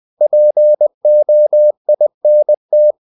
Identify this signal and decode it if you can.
Morse code, heard as a keyed tone, about 20 wpm, 600 Hz. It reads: POINT
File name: the old man and the sea / the old man and the sea part 1 20wpm